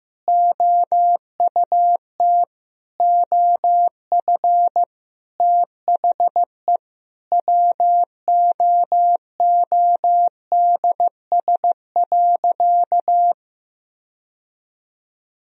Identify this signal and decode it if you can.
Morse code, heard as a keyed tone, about 15 wpm, 700 Hz. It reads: OUT OF THE WOODS.
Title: Sentences from Top 1000 Words - Review Part 2 - 15wpm